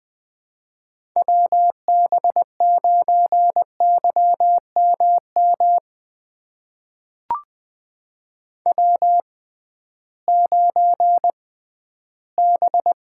Morse code, heard as a keyed tone, about 20 wpm, spaced as for 13 wpm, 700 Hz. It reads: WB9YMM W9B
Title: USA Call Signs - 6x Character Spacing - 20wpm